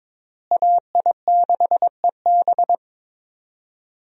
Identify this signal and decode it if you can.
Morse code, heard as a keyed tone, about 22 wpm, 700 Hz. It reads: AI6EB
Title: USA Call Signs - 5 Characters - Repeated 3x - 22wpm